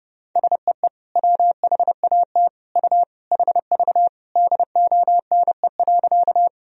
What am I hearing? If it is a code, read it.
Morse code, heard as a keyed tone, about 32 wpm, 700 Hz. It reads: SEE WHAT U HV DONE.
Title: Sentences from Top 500 Words and Abbreviations 30wpm